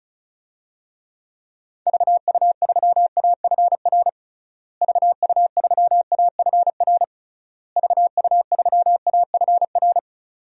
Morse code, heard as a keyed tone, about 36 wpm, 700 Hz. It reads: VU3AFR VU3AFR VU3AFR
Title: DXCC-100 Call Signs - 6 Characters - Repeated 3x - 35wpm